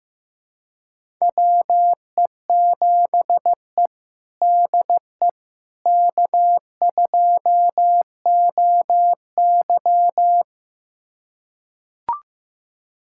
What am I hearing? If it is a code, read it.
Morse code, heard as a keyed tone, about 15 wpm, 700 Hz. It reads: WE7E DE K2OY
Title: USA Callsign DE Callsign - 4 Characters - 15wpm